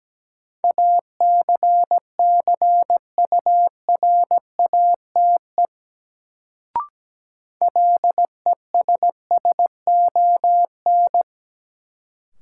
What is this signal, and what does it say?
Morse code, heard as a keyed tone, about 17 wpm, 700 Hz. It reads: ACCURATE LESSON
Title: Sentences from Top 1200 Words - New Word Drill - 17wpm